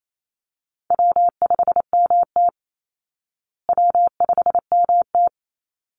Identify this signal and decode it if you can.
Morse code, heard as a keyed tone, about 28 wpm, 700 Hz. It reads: W5MT W5MT
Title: Easy Call Signs Repeated 2x - 28wpm